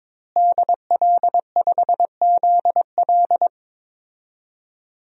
Morse code, heard as a keyed tone, about 22 wpm, 700 Hz. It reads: DL5ZL
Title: DXCC-100 Call Signs - 5 Characters - Repeated 3x - 22wpm